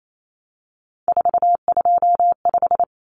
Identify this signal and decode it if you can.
Morse code, heard as a keyed tone, about 28 wpm, 700 Hz. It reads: 425
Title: Numbers - 3-Digit 28wpm